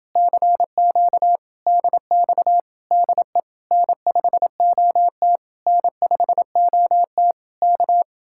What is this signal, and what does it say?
Morse code, heard as a keyed tone, about 28 wpm, 700 Hz. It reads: CQ DX DE N5OT N5OT K